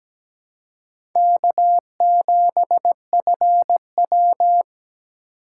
Morse code, heard as a keyed tone, about 17 wpm, 700 Hz. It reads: K7FW
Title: USA Call Signs - 4 Characters - 3x Character Spacing - 17wpm